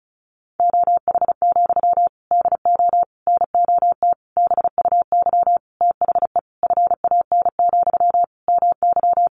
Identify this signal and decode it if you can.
Morse code, heard as a keyed tone, about 34 wpm, 700 Hz. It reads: OH, DO NOT BUY THE FAN, MY
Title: Sentences from Top 1300 Words 35wpm